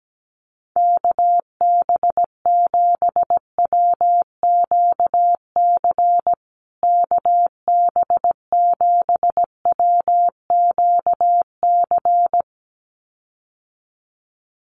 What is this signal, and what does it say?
Morse code, heard as a keyed tone, about 17 wpm, 700 Hz. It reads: KB7WQC KB7WQC
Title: USA Call Signs Repeated 2x 17wpm